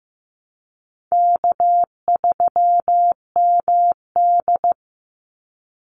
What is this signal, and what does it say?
Morse code, heard as a keyed tone, about 15 wpm, 700 Hz. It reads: K3MD
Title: Easy Call Signs - 15wpm